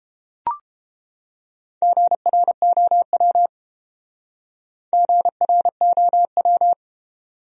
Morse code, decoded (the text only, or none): GROW GROW